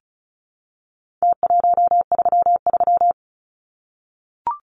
Morse code, heard as a keyed tone, about 34 wpm, 700 Hz. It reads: T133
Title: Numbers - Contest Style - 4-Digit 35wpm